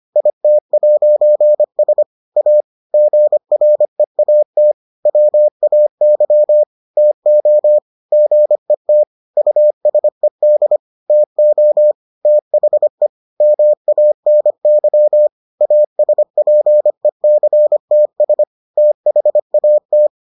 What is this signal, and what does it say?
Morse code, heard as a keyed tone, about 25 wpm, 600 Hz. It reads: IT'S A GREAT WAY TO GET USED TO THE MANY ASPECTS THAT